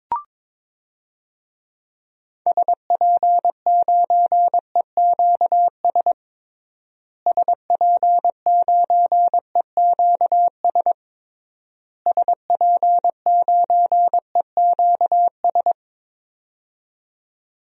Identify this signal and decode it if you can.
Morse code, heard as a keyed tone, about 22 wpm, 700 Hz. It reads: SP9EQS SP9EQS SP9EQS
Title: DXCC-100 Call Signs - 6 Characters - Repeated 3x - 22wpm